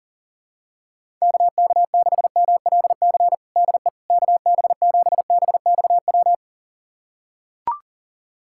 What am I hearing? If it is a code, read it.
Morse code, heard as a keyed tone, about 40 wpm, 700 Hz. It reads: KK6MLC DE KB7BXW
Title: USA Callsign DE Callsign 40wpm